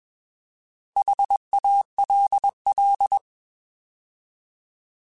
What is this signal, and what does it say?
Morse code, heard as a keyed tone, about 21 wpm, 800 Hz. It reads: HALL
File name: Alphabetical / Alphabetical mp3s / Hs @20